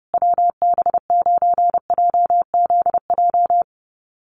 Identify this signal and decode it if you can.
Morse code, heard as a keyed tone, about 30 wpm, 700 Hz. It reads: WB9JZJ
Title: USA Call Signs Repeated 3x 30wpm